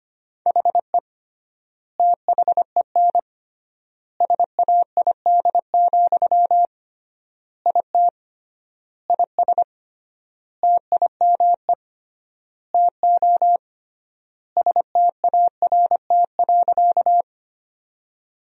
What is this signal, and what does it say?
Morse code, heard as a keyed tone, about 25 wpm, 700 Hz. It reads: HE THEN SAID, IT IS TIME TO START.
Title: 3x Word-Spacing - Sentences from Top 300 Words - 25wpm